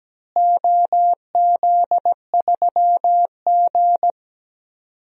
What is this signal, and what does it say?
Morse code, heard as a keyed tone, about 17 wpm, 700 Hz. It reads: OZ3G
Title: DXCC-100 Call Signs - 4 Characters - Repeated 2x - 17wpm